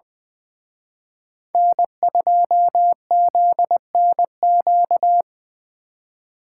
Morse code, heard as a keyed tone, about 24 wpm, 700 Hz. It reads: N2ZNQ
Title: Speed Racing - USA Call Signs 20wpm